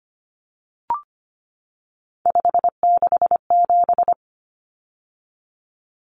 Morse code, heard as a keyed tone, about 25 wpm, 700 Hz. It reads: 567
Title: Numbers - 3-Digit 25wpm